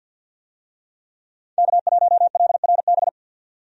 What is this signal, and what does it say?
Morse code, heard as a keyed tone, about 52 wpm, 700 Hz. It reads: K1LRB
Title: USA Call Signs - 5 Characters - 50wpm